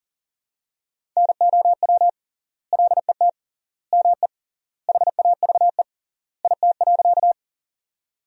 Morse code, heard as a keyed tone, about 40 wpm, 700 Hz. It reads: NOW LET ME HAVE IT.
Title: 3x Word-Spacing - Sentences from Top 300 Words - 40wpm